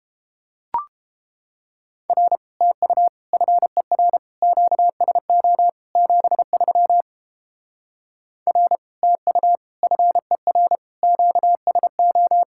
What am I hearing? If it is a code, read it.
Morse code, heard as a keyed tone, about 31 wpm, 700 Hz. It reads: R TU FER QSO 73 R TU FER QSO